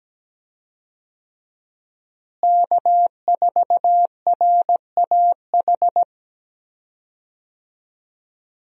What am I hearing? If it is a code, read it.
Morse code, heard as a keyed tone, about 17 wpm, 700 Hz. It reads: K4RAH
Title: USA Call Signs - 5 Characters - Repeated 3x - 17wpm